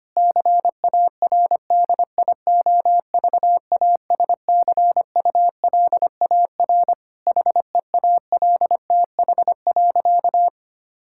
Morse code, heard as a keyed tone, about 25 wpm, 700 Hz. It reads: CARDIOVASCULAR HEALTH.
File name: practice / 25wpm / Difficult Sentences with Hints - 25wpm